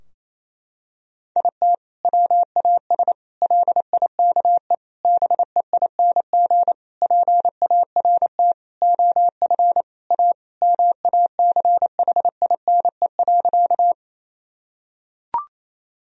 Morse code, heard as a keyed tone, about 28 wpm, 700 Hz. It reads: IT WAS LIKE BEING PART OF A MACHINE.
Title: Sentences from Top 1000 Words - Review Part 3 - 28wpm